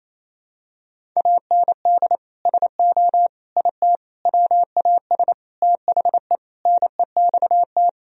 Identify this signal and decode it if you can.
Morse code, heard as a keyed tone, about 28 wpm, 700 Hz. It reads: AND SO IT WAS THE NEXT
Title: Sentences from Top 1000 Words - Review Part 1 - 28wpm